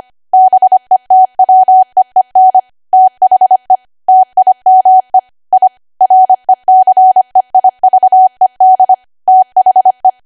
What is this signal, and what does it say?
Morse code, heard as a keyed tone, about 25 wpm, 750 Hz. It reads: BETWEEN THE TIME I RECEIVED THE
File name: Archive / 25 WPM / 25WPM